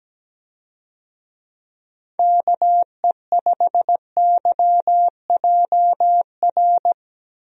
Morse code, heard as a keyed tone, about 17 wpm, 700 Hz. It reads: KE5YJR
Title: USA Call Signs - 6 Characters - Repeated 3x - 17wpm